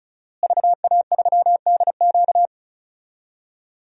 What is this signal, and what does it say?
Morse code, heard as a keyed tone, about 36 wpm, 700 Hz. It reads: VA3DQ
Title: DXCC-100 Call Signs - 5 Characters - Repeated 3x - 35wpm